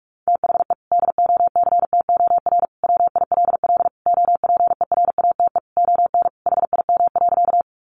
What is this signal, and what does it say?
Morse code, heard as a keyed tone, about 44 wpm, 700 Hz. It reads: THE DOCTOR WILL OPERATE ON HIM.
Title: Sentences from Top 1500 Words 45wpm